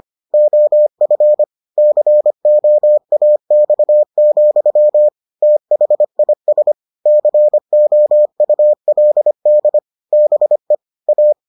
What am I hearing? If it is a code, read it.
Morse code, heard as a keyed tone, about 25 wpm, 600 Hz. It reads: OF COAX, THIS COULD BE A